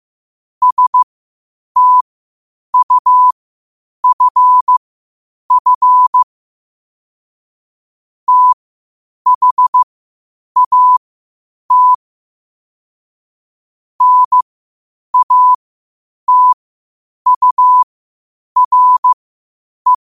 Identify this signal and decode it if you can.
Morse code, heard as a keyed tone, about 15 wpm, spaced as for 8 wpm, 1000 Hz. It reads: STUFF THAT NATURE
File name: qotdfiles / 8wpm